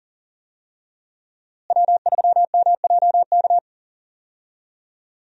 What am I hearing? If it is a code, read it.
Morse code, heard as a keyed tone, about 39 wpm, 700 Hz. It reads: W3MJK
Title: USA Call Signs - 5 Characters - Repeated 2x - 40wpm